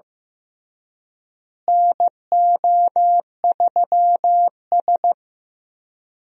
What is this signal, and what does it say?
Morse code, heard as a keyed tone, about 15 wpm, 700 Hz. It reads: NO3S
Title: USA Call Signs - 4 Characters - Repeated 3x - 15wpm